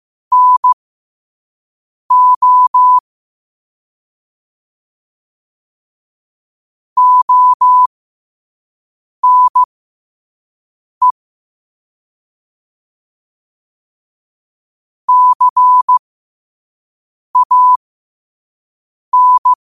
Morse code, heard as a keyed tone, about 15 wpm, spaced as for 5 wpm, 1000 Hz. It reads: NO ONE CAN